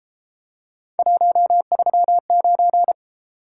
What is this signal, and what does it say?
Morse code, heard as a keyed tone, about 31 wpm, 700 Hz. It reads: 139